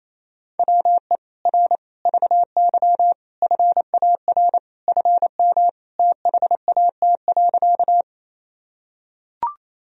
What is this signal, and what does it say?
Morse code, heard as a keyed tone, about 28 wpm, 700 Hz. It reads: WE R VY FAR FM THAT.
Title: Sentences from Top 300 Words and Abbreviations 28wpm